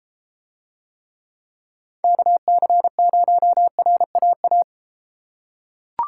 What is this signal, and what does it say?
Morse code, heard as a keyed tone, about 33 wpm, 700 Hz. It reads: KC0RAA